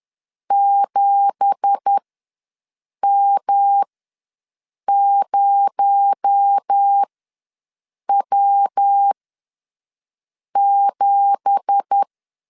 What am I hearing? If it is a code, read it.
Morse code, heard as a keyed tone, about 10 wpm, spaced as for 6 wpm, 800 Hz. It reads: 7M0W7
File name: LEZIONI / 9 2867 ws 10wpm 4215